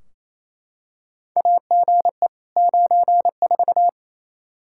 Morse code, heard as a keyed tone, about 27 wpm, 700 Hz. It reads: AGE 94
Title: K1USN SST QSO Practice 28wpm